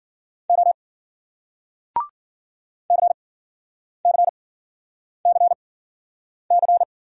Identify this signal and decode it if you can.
Morse code, heard as a keyed tone, about 53 wpm, spaced as for 9 wpm, 700 Hz. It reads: O CCCC